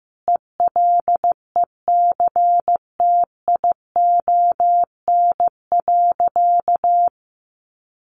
Morse code, heard as a keyed tone, about 15 wpm, 700 Hz. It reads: ELECTION.
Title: Sentences from Top 1500 Words 15wpm